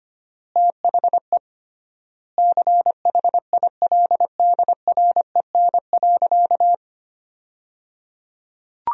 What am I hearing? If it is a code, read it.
Morse code, heard as a keyed tone, about 25 wpm, 700 Hz. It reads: THE CHILDREN.
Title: 3x Word-Spacing - Sentences from Top 300 Words - 25wpm